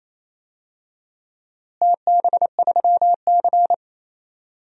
Morse code, heard as a keyed tone, about 28 wpm, 700 Hz. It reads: TB3C
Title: DXCC-100 Call Signs - 4 Characters - Repeated 2x - 28wpm